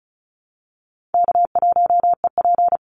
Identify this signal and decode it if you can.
Morse code, heard as a keyed tone, about 34 wpm, 700 Hz. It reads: K1EP
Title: Call Signs - 35wpm